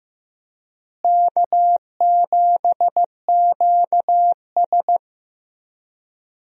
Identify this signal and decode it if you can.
Morse code, heard as a keyed tone, about 16 wpm, 700 Hz. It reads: K7QS
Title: USA Call Signs - 4 Characters - Repeated 3x - 15wpm